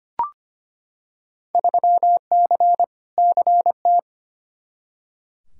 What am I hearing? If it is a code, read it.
Morse code, heard as a keyed tone, about 25 wpm, 700 Hz. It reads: 3C CT
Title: Field Day Exchanges Encore - 25wpm